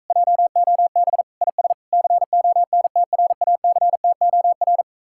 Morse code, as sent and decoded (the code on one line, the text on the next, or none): .--- --- -... / .. ... / -.-. --- -. - .-. .- -.-. - --- .-.
JOB IS CONTRACTOR